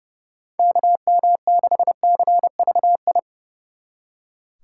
Morse code, heard as a keyed tone, about 40 wpm, 700 Hz. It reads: KM6CVI